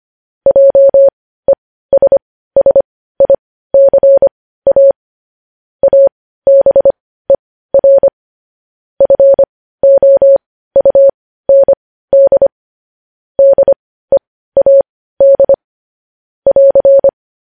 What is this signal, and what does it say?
Morse code, heard as a keyed tone, about 25 wpm, spaced as for 15 wpm, 550 Hz. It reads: JESSICA ABER FOUND DEAD <AR>
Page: news-headlines-in-morse-code-at-15-wpm